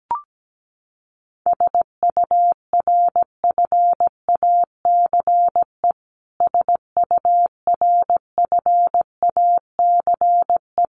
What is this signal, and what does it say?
Morse code, heard as a keyed tone, about 17 wpm, 700 Hz. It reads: SURFACE SURFACE
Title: Top 500 Words Repeated 17wpm